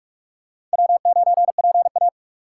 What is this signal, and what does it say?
Morse code, heard as a keyed tone, about 44 wpm, 700 Hz. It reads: W9PA